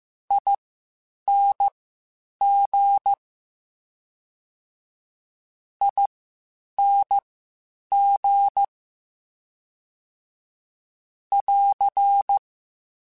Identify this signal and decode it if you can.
Morse code, heard as a keyed tone, about 15 wpm, spaced as for 8 wpm, 800 Hz. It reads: ING ING <AR>
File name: General / General-mp3s / Suffixes sp @13